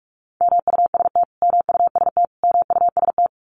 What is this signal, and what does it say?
Morse code, heard as a keyed tone, about 44 wpm, 700 Hz. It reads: MUST MUST MUST
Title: Top 200 Words Repeated 45wpm